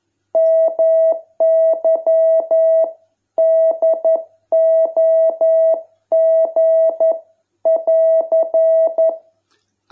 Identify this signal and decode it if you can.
Morse code, heard as a keyed tone, about 11 wpm, 660 Hz. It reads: MY DOG <AR>